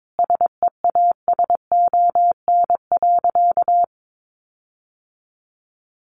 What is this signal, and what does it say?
Morse code, heard as a keyed tone, about 22 wpm, 700 Hz. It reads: SEASON.
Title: Sentences from Top 1400 Words 22wpm